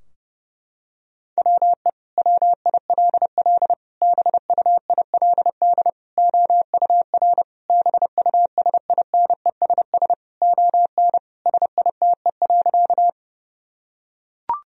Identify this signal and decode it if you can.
Morse code, heard as a keyed tone, about 32 wpm, 700 Hz. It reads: WE WILL BUILD OUR BUSINESS ON SITE.
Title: Sentences from Top 1000 Words - Review Part 4 - 30wpm